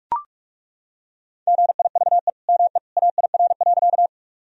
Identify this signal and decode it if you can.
Morse code, heard as a keyed tone, about 43 wpm, 700 Hz. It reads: GIVE ME AIR.
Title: Sentences from Top 1000 Words - Review Part 1 - 45wpm